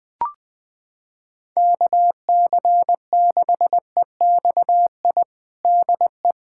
Morse code, heard as a keyed tone, about 21 wpm, 700 Hz. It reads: KC6EXI DE
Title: USA Callsign DE Callsign 20wpm